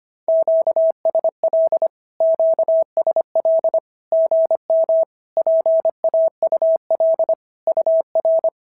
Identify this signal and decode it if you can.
Morse code, heard as a keyed tone, about 25 wpm, 650 Hz. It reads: QSL QSL GM PAUL UR
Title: SOTA QSO Practice 25wpm